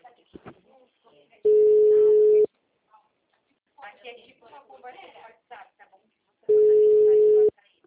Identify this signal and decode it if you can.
Ringback tone (Europe, 425 Hz), with speech.
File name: out- -2036- - - .19568